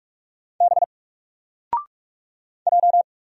Morse code, heard as a keyed tone, about 44 wpm, 700 Hz. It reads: B J
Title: Single Letter-Number Alt 45wpm